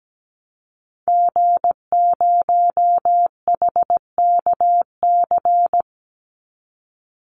Morse code, heard as a keyed tone, about 17 wpm, 700 Hz. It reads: G0HKC
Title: Call Signs Repeated 2x - 17wpm